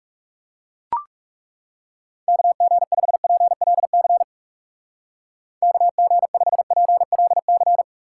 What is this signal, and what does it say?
Morse code, heard as a keyed tone, about 37 wpm, 700 Hz. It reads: KG5PLC KG5PLC
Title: Speed Racing - USA 2x3 Call Signs 30wpm